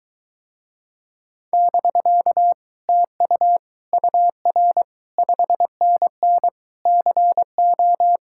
Morse code, heard as a keyed tone, about 24 wpm, 700 Hz. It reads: <BK> TU UR 5NN CO